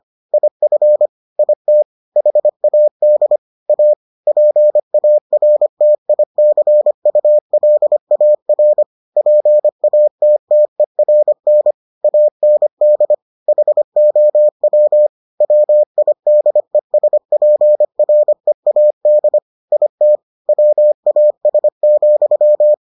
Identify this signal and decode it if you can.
Morse code, heard as a keyed tone, about 25 wpm, 600 Hz. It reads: IF IT HAD A PARTICULAR PATTERN AND HOW WIDESPREAD IT WAS,